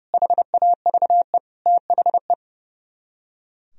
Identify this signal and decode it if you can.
Morse code, heard as a keyed tone, about 40 wpm, 700 Hz. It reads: HAVE THE